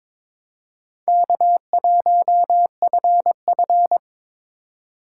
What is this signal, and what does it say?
Morse code, heard as a keyed tone, about 22 wpm, 700 Hz. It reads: K1FF
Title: USA Call Signs - 3x Character Spacing - 22wpm